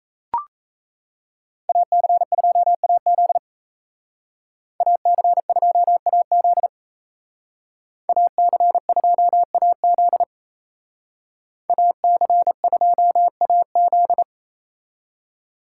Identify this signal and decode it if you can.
Morse code, heard as a keyed tone, about 34 wpm, 700 Hz. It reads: AC2AZ AC2AZ AC2AZ AC2AZ
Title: Speed Racing - USA 2x2 Call Signs 28wpm